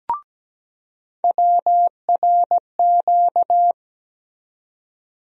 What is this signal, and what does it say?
Morse code, heard as a keyed tone, about 17 wpm, 700 Hz. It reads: WRQ
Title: USA Call Sign Postfixes - 3 Letters 17wpm